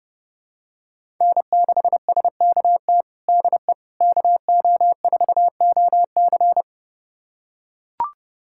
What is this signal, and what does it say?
Morse code, heard as a keyed tone, about 32 wpm, 700 Hz. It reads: N6SKT DE KO4OC
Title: USA Callsign DE Callsign - 5 Characters - 30wpm